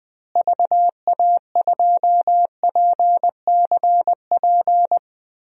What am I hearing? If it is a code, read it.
Morse code, heard as a keyed tone, about 20 wpm, 700 Hz. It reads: VA2PCP
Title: DXCC-100 Call Signs - 6 Characters - Repeated 3x - 20wpm